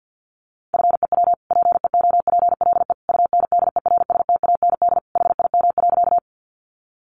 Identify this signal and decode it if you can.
Morse code, heard as a keyed tone, about 50 wpm, 700 Hz. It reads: FEW PEOPLE UNDERSTAND HIM.
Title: Sentences from Top 600 Words 50wpm